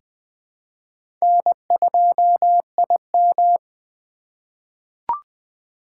Morse code, heard as a keyed tone, about 20 wpm, 700 Hz. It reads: N2IM
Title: USA Call Signs - 4 Characters - 3x Character Spacing - 20wpm